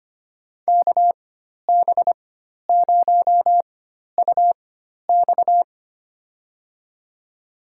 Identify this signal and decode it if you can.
Morse code, heard as a keyed tone, about 25 wpm, spaced as for 12 wpm, 700 Hz. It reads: KB0UX
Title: USA Call Signs - 5 Characters - 4x Character Spacing - 25wpm